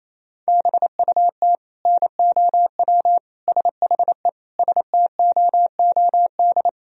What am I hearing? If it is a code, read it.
Morse code, heard as a keyed tone, about 28 wpm, 700 Hz. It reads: BUT NOW SHE STOOD
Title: Sentences from Top 1000 Words - Review Part 2 - 28wpm